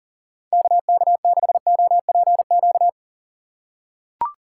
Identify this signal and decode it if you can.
Morse code, heard as a keyed tone, about 40 wpm, 690 Hz. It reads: KK6OPQ
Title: USA Call Signs - 2x Character Spacing - 40wpm